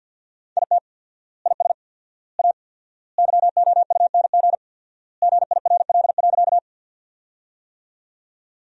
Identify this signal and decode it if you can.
Morse code, heard as a keyed tone, about 50 wpm, 700 Hz. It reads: IT IS A YOUNG GIRL.
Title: 4x Word-Spacing - Sentences from Top 400 Words - 50wpm